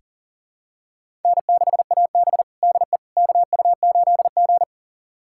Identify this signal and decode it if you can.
Morse code, heard as a keyed tone, about 40 wpm, 700 Hz. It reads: N6AB DE KU8G
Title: USA Callsign DE Callsign - 4 Characters - 40wpm